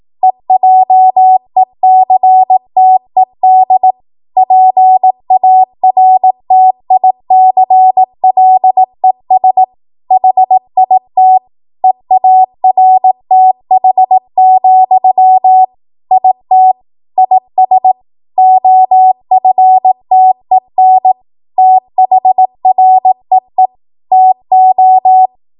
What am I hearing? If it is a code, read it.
Morse code, heard as a keyed tone, about 18 wpm, 750 Hz. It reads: EJECTED PARTICLES HIT EARTH, IT IS OFTEN THREE TO
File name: Morse / 18WPM